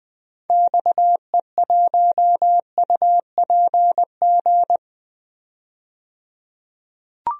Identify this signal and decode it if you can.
Morse code, heard as a keyed tone, about 20 wpm, 700 Hz. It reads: XE1UPG
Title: DXCC-100 Call Signs - 6 Characters - 20wpm